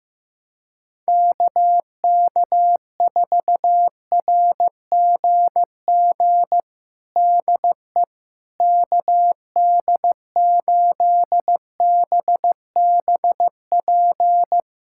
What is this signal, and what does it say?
Morse code, heard as a keyed tone, about 15 wpm, 700 Hz. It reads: KK4RGG DE KD8BBP
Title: USA Callsign DE Callsign 15wpm